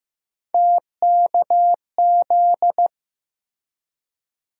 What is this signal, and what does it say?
Morse code, heard as a keyed tone, about 15 wpm, 700 Hz. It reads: TKZ
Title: USA Call Sign Postfixes - 3 Letters 15wpm